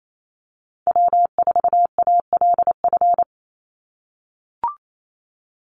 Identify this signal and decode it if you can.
Morse code, heard as a keyed tone, about 28 wpm, 700 Hz. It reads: W4ALF
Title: Call Signs - 28wpm